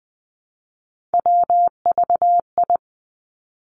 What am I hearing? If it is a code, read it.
Morse code, heard as a keyed tone, about 20 wpm, 700 Hz. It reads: WVI
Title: Three Challenging Letter-Number Combinations 20wpm